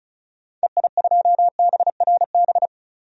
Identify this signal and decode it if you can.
Morse code, heard as a keyed tone, about 36 wpm, 690 Hz. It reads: EI2BRB
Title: DXCC-100 Call Signs - 6 Characters - 35wpm